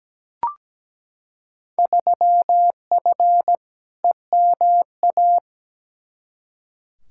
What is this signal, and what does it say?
Morse code, heard as a keyed tone, about 17 wpm, 700 Hz. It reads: 3F EMA
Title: Field Day Exchanges Encore - 17wpm